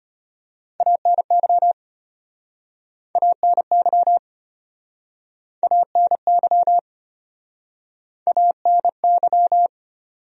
Morse code, heard as a keyed tone, about 36 wpm, 700 Hz. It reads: ANY ANY ANY ANY